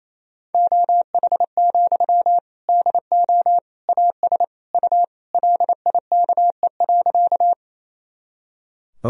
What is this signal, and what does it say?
Morse code, heard as a keyed tone, about 28 wpm, 700 Hz. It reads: OH, DO AS U LIKE.
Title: Sentences from Top 500 Words and Abbreviations 28wpm